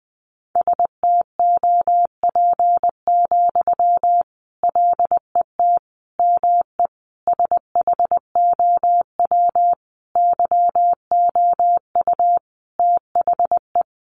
Morse code, heard as a keyed tone, about 19 wpm, 700 Hz. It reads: STOP, LET ME SHOW YOU THE
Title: Sentences from Top 300 Words 20wpm